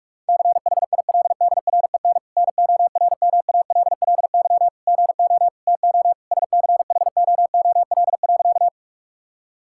Morse code, heard as a keyed tone, about 46 wpm, 700 Hz. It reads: CHILDREN NORMALLY GO TO SCHOOL.
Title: Sentences from Top 1000 Words - Review Part 6 - 45wpm